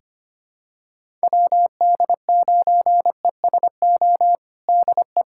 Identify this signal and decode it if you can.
Morse code, heard as a keyed tone, about 25 wpm, 700 Hz. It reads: WD9ESO DE
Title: USA Callsign DE Callsign 25wpm